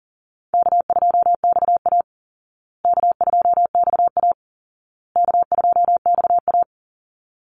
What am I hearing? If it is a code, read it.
Morse code, heard as a keyed tone, about 40 wpm, 700 Hz. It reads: K2XA K2XA K2XA
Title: Easy Call Signs Repeated 3x - 40wpm